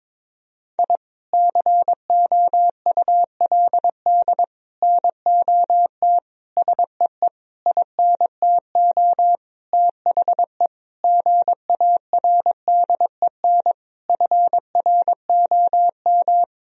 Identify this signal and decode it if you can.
Morse code, heard as a keyed tone, about 22 wpm, 700 Hz. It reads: I COULD NOT SEE INTO THE GARDEN FROM
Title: Sentences from Top 1000 Words - Review Part 4 - 22wpm